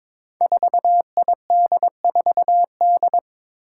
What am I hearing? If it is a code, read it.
Morse code, heard as a keyed tone, about 23 wpm, 700 Hz. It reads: 4ID4D
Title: Five Challenging Letter-Number Combinations 22wpm